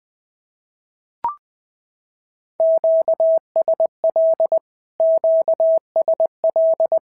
Morse code, heard as a keyed tone, about 22 wpm, 650 Hz. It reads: QSL QSL
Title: SOTA QSO Practice 20wpm